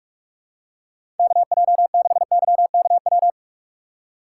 Morse code, heard as a keyed tone, about 42 wpm, 700 Hz. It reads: KJ6YKW
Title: USA Call Signs - 6 Characters - 6x Character Spacing - 45wpm